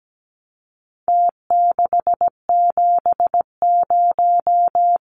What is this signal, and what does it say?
Morse code, heard as a keyed tone, about 17 wpm, 700 Hz. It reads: T670
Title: Numbers - Contest Style - 4-Digit 17wpm